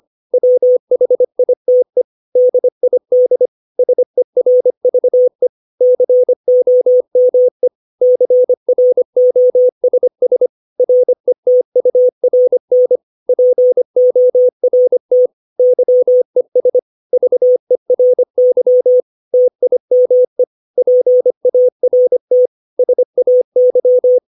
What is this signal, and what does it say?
Morse code, heard as a keyed tone, about 25 wpm, 500 Hz. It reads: WHITE DID SERVE COME CROSS RETURN PORT YES VERY TIME PART SAY
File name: 632 words / 632 Words 25WPM